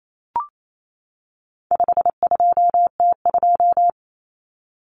Morse code, heard as a keyed tone, about 28 wpm, 700 Hz. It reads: 52T2
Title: Common Cut Numbers - 4-Digit 28wpm